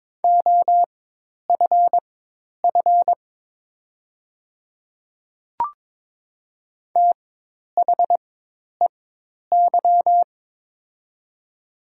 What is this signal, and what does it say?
Morse code, heard as a keyed tone, about 22 wpm, spaced as for 10 wpm, 700 Hz. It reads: OFF THEY
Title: 4x Character Spacing - Top 300 Words - 22wpm